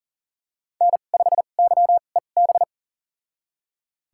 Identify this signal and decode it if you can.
Morse code, heard as a keyed tone, about 39 wpm, spaced as for 29 wpm, 700 Hz. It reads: N5YEB
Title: USA Call Signs - 5 Characters - 2x Character Spacing - 40wpm